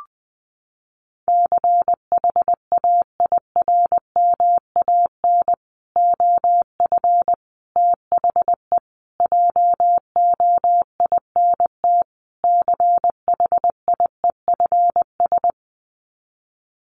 Morse code, heard as a keyed tone, about 20 wpm, 700 Hz. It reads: CHAIRMAN OF THE JOINT CHIEFS
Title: Sets of 5 Words Encore 20wpm